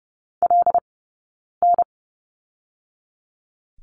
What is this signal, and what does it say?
Morse code, heard as a keyed tone, about 30 wpm, spaced as for 9 wpm, 690 Hz. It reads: LN
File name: course / 30:10wpm / 086 Lesson 15d - Number 5 - Two Characters 30wpm